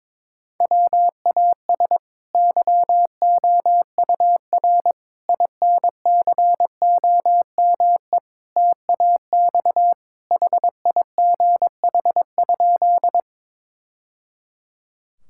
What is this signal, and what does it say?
Morse code, heard as a keyed tone, about 22 wpm, 700 Hz. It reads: WAS YOUR INCOME TAX HIGH?
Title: Sentences from Top 1000 Words - Review Part 5 - 22wpm